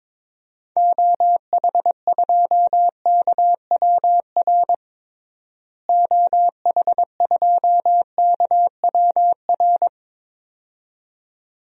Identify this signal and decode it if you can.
Morse code, heard as a keyed tone, about 22 wpm, 700 Hz. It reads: OH2KWR OH2KWR
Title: DXCC-100 Call Signs - 6 Characters - Repeated 2x - 22wpm